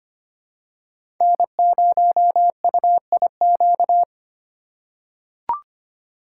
Morse code, heard as a keyed tone, about 25 wpm, 700 Hz. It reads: N0UIQ
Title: USA Call Signs - 5 Characters - 3x Character Spacing - 25wpm